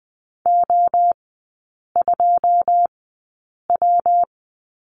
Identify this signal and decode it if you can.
Morse code, heard as a keyed tone, about 20 wpm, spaced as for 8 wpm, 690 Hz. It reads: O2W
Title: 145 Lesson 24e - Letter y - Three Characters 20wpm